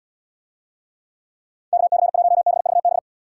Morse code, heard as a keyed tone, about 68 wpm, 700 Hz. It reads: KC1BVB